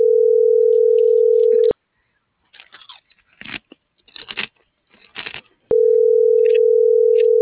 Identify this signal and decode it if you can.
Ringback tone (North America), 440 + 480 Hz.